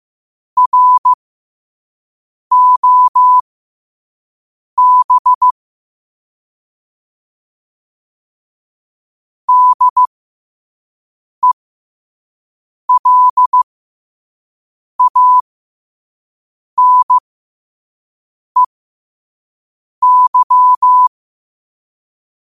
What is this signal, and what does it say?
Morse code, heard as a keyed tone, about 15 wpm, spaced as for 5 wpm, 1000 Hz. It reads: ROB DELANEY
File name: qotdfiles / 5wpm